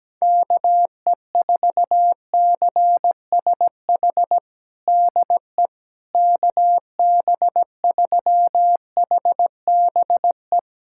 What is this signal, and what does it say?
Morse code, heard as a keyed tone, about 17 wpm, 700 Hz. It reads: KE4CSH DE KB3HBE
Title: USA Callsign DE Callsign 17wpm